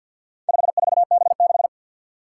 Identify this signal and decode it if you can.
Morse code, heard as a keyed tone, about 48 wpm, 700 Hz. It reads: 54B6
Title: Four Challenging Letter-Number Combinations 50wpm